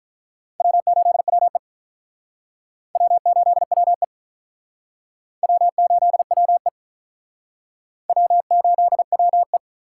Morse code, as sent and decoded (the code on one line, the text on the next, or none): .-- ---.. .-- . / .-- ---.. .-- . / .-- ---.. .-- . / .-- ---.. .-- .
W8WE W8WE W8WE W8WE